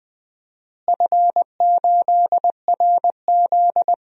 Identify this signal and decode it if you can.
Morse code, heard as a keyed tone, about 19 wpm, 700 Hz. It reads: F8RZ
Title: DXCC-100 Call Signs - 4 Characters - Repeated 3x - 20wpm